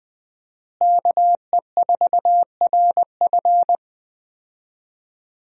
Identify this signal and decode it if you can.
Morse code, heard as a keyed tone, about 20 wpm, 700 Hz. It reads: KE4RF
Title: USA Call Signs - 5 Characters - Repeated 3x - 20wpm